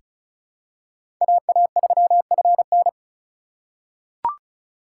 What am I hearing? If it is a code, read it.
Morse code, heard as a keyed tone, about 34 wpm, 700 Hz. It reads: AA3FN
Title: USA Call Signs - 4x Character Spacing - 35wpm